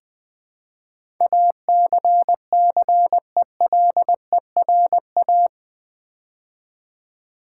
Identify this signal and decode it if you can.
Morse code, heard as a keyed tone, about 20 wpm, 700 Hz. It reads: ACCELERA
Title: Build A Word - 4 Letters At A Time - 10-Letter Words - 20wpm